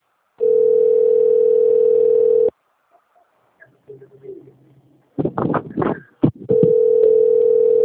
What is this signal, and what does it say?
Ringback tone (North America), 440 + 480 Hz.